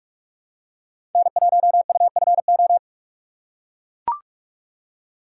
Morse code, heard as a keyed tone, about 41 wpm, 700 Hz. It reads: N1UFO
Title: USA Call Signs - 5 Characters - 3x Character Spacing - 45wpm